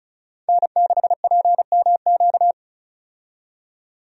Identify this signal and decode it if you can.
Morse code, heard as a keyed tone, about 34 wpm, 700 Hz. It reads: N6PMQ
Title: USA Call Signs - 5 Characters - 35wpm